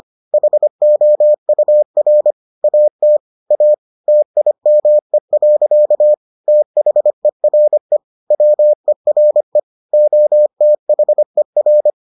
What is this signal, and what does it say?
Morse code, heard as a keyed tone, about 25 wpm, 600 Hz. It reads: HOUR AT A TIME. THERE WERE OTHER